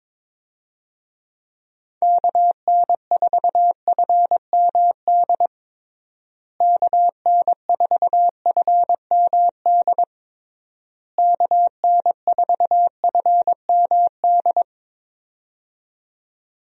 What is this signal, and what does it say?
Morse code, heard as a keyed tone, about 22 wpm, 700 Hz. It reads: KN4FMD KN4FMD KN4FMD
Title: USA Call Signs - 6 Characters - Repeated 3x - 22wpm